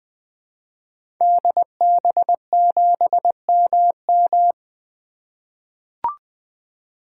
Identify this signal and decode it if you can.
Morse code, heard as a keyed tone, about 20 wpm, 700 Hz. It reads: DB7MM
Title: 2023 SOTA Activator Call Signs - 20wpm